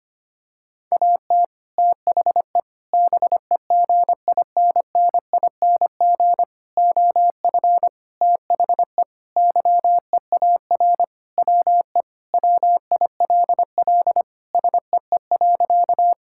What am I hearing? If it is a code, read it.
Morse code, heard as a keyed tone, about 25 wpm, 700 Hz. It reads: AT THE BEGINNING OF THE YEAR WE WILL SEE.
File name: practice / 25wpm / Sentences with 3-Word Phrases - 25wpm